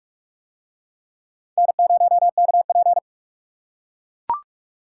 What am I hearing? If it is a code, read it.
Morse code, heard as a keyed tone, about 43 wpm, 700 Hz. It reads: N0KP